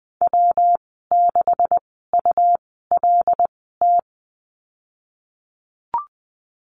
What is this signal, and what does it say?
Morse code, heard as a keyed tone, about 20 wpm, spaced as for 15 wpm, 700 Hz. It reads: W6ULT
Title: 147 Lesson 24g - Letter y - Call Signs 20wpm